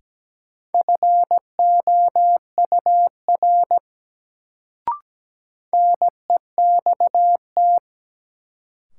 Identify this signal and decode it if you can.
Morse code, heard as a keyed tone, about 17 wpm, 700 Hz. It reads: FOUR NEXT
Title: Sentences from Top 300 Words - New Word Drill - 17wpm